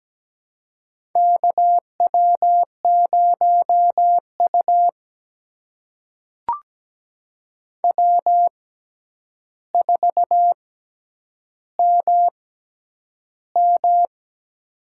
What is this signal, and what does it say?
Morse code, heard as a keyed tone, about 17 wpm, spaced as for 9 wpm, 700 Hz. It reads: KW0U W4MM
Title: USA Call Signs - 4 Characters - 6x Character Spacing - 17wpm